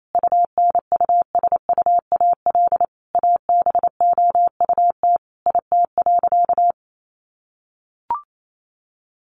Morse code, heard as a keyed tone, about 28 wpm, 700 Hz. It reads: UNUSUAL ABOUT IT.
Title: Sentences from Top 1400 Words 28wpm